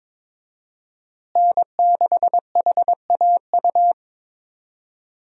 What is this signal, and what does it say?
Morse code, heard as a keyed tone, about 22 wpm, 700 Hz. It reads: N6HAU
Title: USA Call Signs - 5 Characters - Repeated 2x - 22wpm